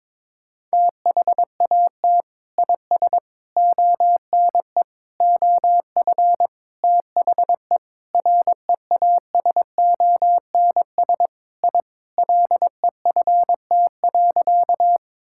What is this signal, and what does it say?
Morse code, heard as a keyed tone, about 22 wpm, 700 Hz. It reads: THAT IS ONE OF THE REASONS I LEFT.
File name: practice / 22wpm / Sentences with 4-Word Phrases - 22wpm